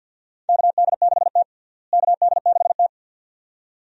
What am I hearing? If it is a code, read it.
Morse code, heard as a keyed tone, about 51 wpm, 700 Hz. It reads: KD6T KD6T
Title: USA Call Signs - 4 Characters - Repeated 2x - 50wpm